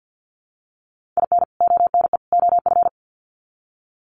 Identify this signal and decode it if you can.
Morse code, heard as a keyed tone, about 48 wpm, 700 Hz. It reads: IN ONE OF
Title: Sets of 3 Words 50wpm